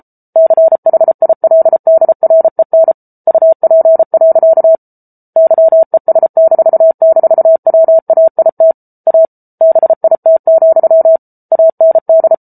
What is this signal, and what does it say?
Morse code, heard as a keyed tone, about 34 wpm, 650 Hz. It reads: CHILDREN UP. YES--WAIT A BIT, AND
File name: Book Two / 07-Chapter 7 - 33 WPM